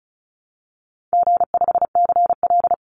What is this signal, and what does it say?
Morse code, heard as a keyed tone, about 34 wpm, 700 Hz. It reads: G5CL
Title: Easy Call Signs Repeated 3x - 35wpm